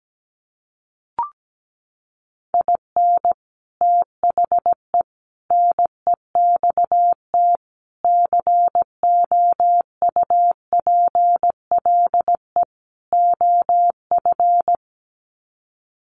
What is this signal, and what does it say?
Morse code, heard as a keyed tone, about 17 wpm, 700 Hz. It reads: IN THE NEXT COUPLE OF
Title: Sets of 5 Words Encore2 17wpm